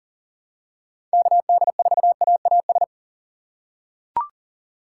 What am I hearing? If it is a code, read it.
Morse code, heard as a keyed tone, about 40 wpm, 700 Hz. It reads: KD4AAS
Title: USA Call Signs - 6 Characters - 3x Character Spacing - 40wpm